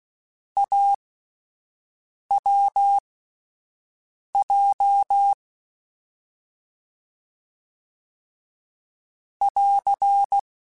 Morse code, heard as a keyed tone, about 16 wpm, spaced as for 5 wpm, 800 Hz. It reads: AWJ <AR>
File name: Difficult Combos / Difficult Combos-mp3s / AWJ @5